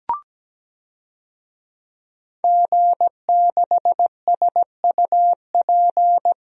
Morse code, heard as a keyed tone, about 17 wpm, 700 Hz. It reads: G6SUP
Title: DXCC-100 Call Signs - 5 Characters - 17wpm